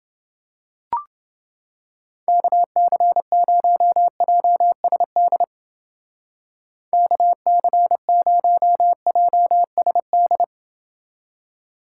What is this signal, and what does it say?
Morse code, heard as a keyed tone, about 29 wpm, 700 Hz. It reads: KC0JSD KC0JSD